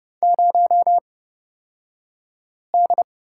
Morse code, heard as a keyed tone, about 31 wpm, spaced as for 5 wpm, 700 Hz. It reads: 0D
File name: Lesson 38d - Letter q - Two Characters 30wpm